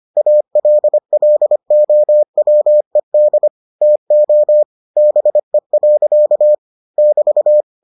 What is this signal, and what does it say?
Morse code, heard as a keyed tone, about 25 wpm, 600 Hz. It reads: ALLOWED TO BE. <BT>